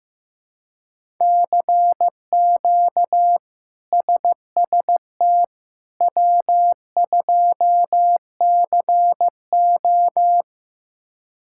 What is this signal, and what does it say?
Morse code, heard as a keyed tone, about 15 wpm, 700 Hz. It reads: CQ SST W2CO
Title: K1USN SST QSO Practice - 15wpm